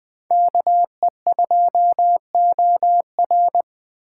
Morse code, heard as a keyed tone, about 20 wpm, 700 Hz. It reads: KE2OR